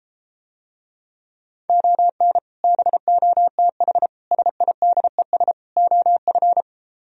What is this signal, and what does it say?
Morse code, heard as a keyed tone, about 33 wpm, 700 Hz. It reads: ON BOTH SIDES OF